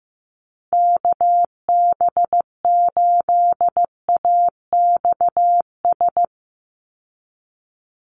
Morse code, heard as a keyed tone, about 15 wpm, 700 Hz. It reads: KB8AXS
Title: USA Call Signs 15wpm